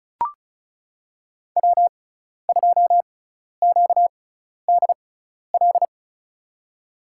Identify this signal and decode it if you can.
Morse code, heard as a keyed tone, about 36 wpm, spaced as for 12 wpm, 700 Hz. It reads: W2QDL
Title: USA Call Signs - 5 Characters - 6x Character Spacing - 35wpm